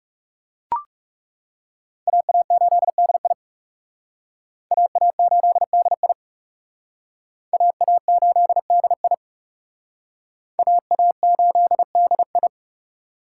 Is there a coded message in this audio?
Yes: Morse code from 2.1 to 12.5 s, heard as a keyed tone, about 40 wpm, 700 Hz. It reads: AA8DI AA8DI AA8DI AA8DI